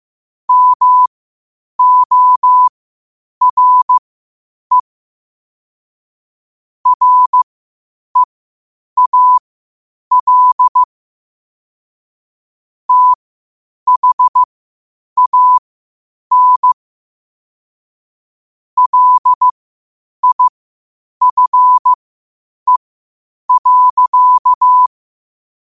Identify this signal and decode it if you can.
Morse code, heard as a keyed tone, about 15 wpm, spaced as for 8 wpm, 1000 Hz. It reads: MORE REAL THAN LIFE.
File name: qotdfiles / 8wpm